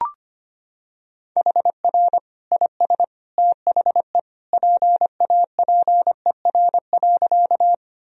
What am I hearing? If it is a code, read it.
Morse code, heard as a keyed tone, about 25 wpm, 700 Hz. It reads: HR IS THE PAPER.
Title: Sentences from Top 300 Words and Abbreviations 25wpm